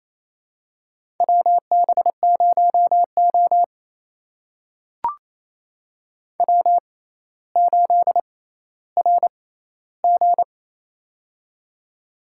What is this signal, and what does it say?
Morse code, heard as a keyed tone, about 28 wpm, spaced as for 14 wpm, 700 Hz. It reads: WB0O W8RG
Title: USA Call Signs - 4 Characters - 6x Character Spacing - 28wpm